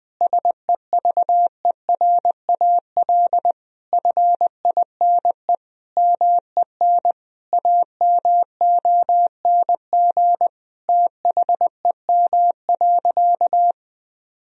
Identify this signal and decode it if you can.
Morse code, heard as a keyed tone, about 20 wpm, 700 Hz. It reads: SEVERAL FINE MEN AMONG THEM.
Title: Sentences from Top 1000 Words - Review Part 2 - 20wpm